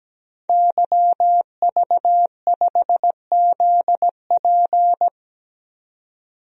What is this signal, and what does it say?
Morse code, heard as a keyed tone, about 17 wpm, 700 Hz. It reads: YV5ZP
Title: DXCC-100 Call Signs - 5 Characters - 17wpm